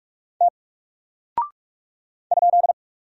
Morse code, heard as a keyed tone, about 46 wpm, 700 Hz. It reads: T ?
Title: Single Character (Letter:Number:Punctuation:Prosign) Alt 45wpm